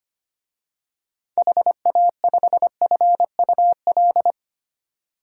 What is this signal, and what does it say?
Morse code, heard as a keyed tone, about 25 wpm, 700 Hz. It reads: HA5FUL
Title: DXCC-100 Call Signs - 6 Characters - 25wpm